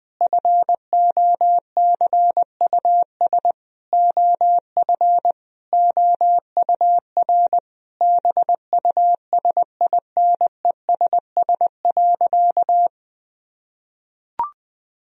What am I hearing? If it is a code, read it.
Morse code, heard as a keyed tone, about 20 wpm, 700 Hz. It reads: FOCUS OF OUR BUSINESS.
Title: Sentences from Top 1000 Words - Review Part 4 - 20wpm